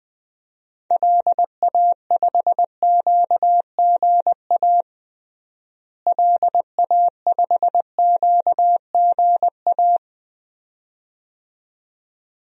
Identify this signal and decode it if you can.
Morse code, heard as a keyed tone, about 20 wpm, 700 Hz. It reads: LA5QGA LA5QGA
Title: DXCC-100 Call Signs - 6 Characters - Repeated 2x - 20wpm